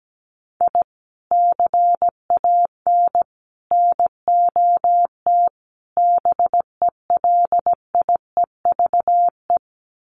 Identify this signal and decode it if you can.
Morse code, heard as a keyed tone, about 17 wpm, 700 Hz. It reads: I CAN NOT BELIEVE
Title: Sets of 4 Words 17wpm